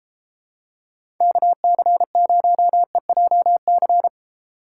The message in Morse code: KC0EJC

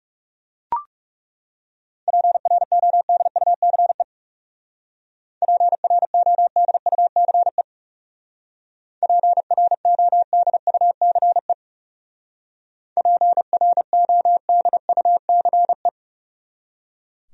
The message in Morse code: PRODUCE PRODUCE PRODUCE PRODUCE

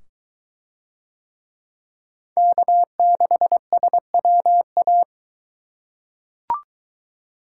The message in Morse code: K6SWA

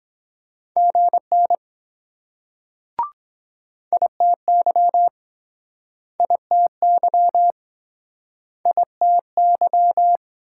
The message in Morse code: GN ITY ITY ITY

